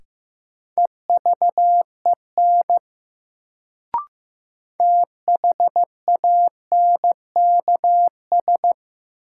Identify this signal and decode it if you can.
Morse code, heard as a keyed tone, about 15 wpm, 700 Hz. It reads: EVEN THANKS
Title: Sentences from Top 600 Words - New Word Drill - 15wpm